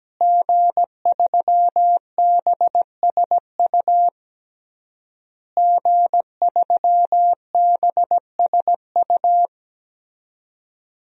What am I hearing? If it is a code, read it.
Morse code, heard as a keyed tone, about 17 wpm, 700 Hz. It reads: G3BSU G3BSU
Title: DXCC-100 Call Signs - 5 Characters - Repeated 3x - 17wpm